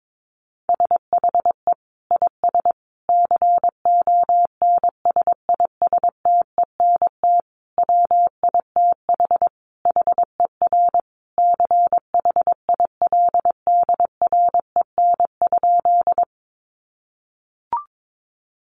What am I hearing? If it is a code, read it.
Morse code, heard as a keyed tone, about 22 wpm, 700 Hz. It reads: SHE IS CONSISTENT WITH HER CHILDREN?
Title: Sentences from Top 1500 Words 22wpm